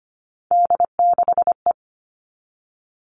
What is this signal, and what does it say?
Morse code, heard as a keyed tone, about 25 wpm, 700 Hz. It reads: D6E
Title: All Cut Numbers - 3-Digit 25wpm